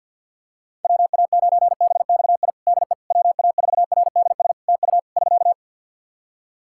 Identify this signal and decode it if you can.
Morse code, heard as a keyed tone, about 51 wpm, 700 Hz. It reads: WA9BXI DE WA4RDS TU <SK>
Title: Practice QSO - Exchange 4 - Ending - 50wpm